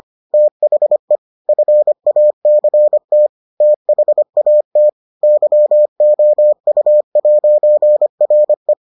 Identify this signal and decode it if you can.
Morse code, heard as a keyed tone, about 25 wpm, 600 Hz. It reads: THE FACT THAT YOU'RE